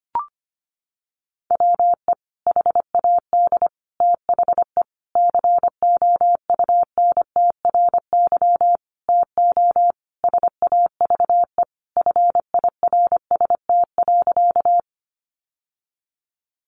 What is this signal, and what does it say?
Morse code, heard as a keyed tone, about 25 wpm, 700 Hz. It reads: WE HAD THE COUNTRY TO SAVE FIRST.
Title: Sentences from Top 1100 Words 25wpm